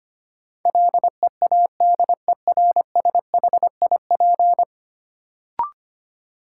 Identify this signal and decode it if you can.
Morse code, heard as a keyed tone, about 25 wpm, 700 Hz. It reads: LEADERSHIP
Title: Sentences from Top 1600 Words - New Word Drill - 25wpm